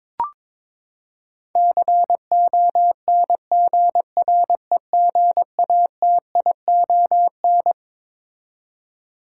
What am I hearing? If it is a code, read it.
Morse code, heard as a keyed tone, about 22 wpm, 700 Hz. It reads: CONGREGATION
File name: Twelve Letter Words - 22wpm